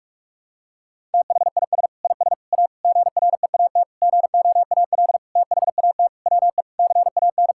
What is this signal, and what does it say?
Morse code, heard as a keyed tone, about 46 wpm, 700 Hz. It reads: THIS IS A GREAT GOAL THAT WE CAN
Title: Sentences from Top 1000 Words - Review Part 6 - 45wpm